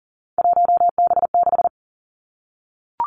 Morse code, heard as a keyed tone, about 39 wpm, 700 Hz. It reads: 1B6
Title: Three Challenging Letter-Number Combinations 40wpm